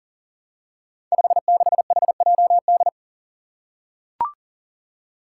Morse code, heard as a keyed tone, about 39 wpm, 700 Hz. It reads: 56HJD
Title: Five Challenging Letter-Number Combinations 40wpm